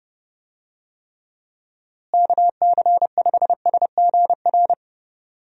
Morse code, heard as a keyed tone, about 32 wpm, 700 Hz. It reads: KC5SGR
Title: USA Call Signs - 6 Characters - Repeated 3x - 30wpm